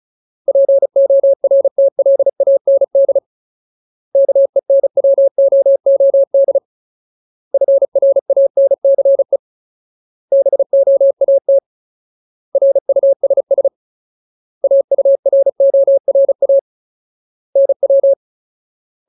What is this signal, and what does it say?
Morse code, heard as a keyed tone, about 34 wpm, 550 Hz. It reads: PORTLAND KENWOOD FRANCE BOAT RUSS AURORA NW
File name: Ultimate Word List / 35WPM Ultimate